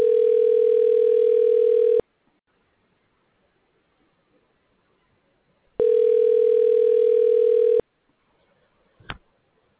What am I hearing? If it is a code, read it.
Ringback tone (North America), 440 + 480 Hz.
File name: recordings / RE f661dad17dbc1d2ce92095f122